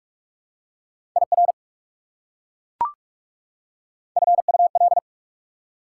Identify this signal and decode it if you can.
Morse code, heard as a keyed tone, about 43 wpm, 690 Hz. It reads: IR FUL
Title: Prefixes and Suffixes 45wpm